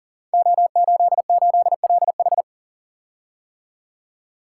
Morse code, heard as a keyed tone, about 39 wpm, 690 Hz. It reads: O88LH
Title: Code Groups - 5 Letter:Numbers - 40wpm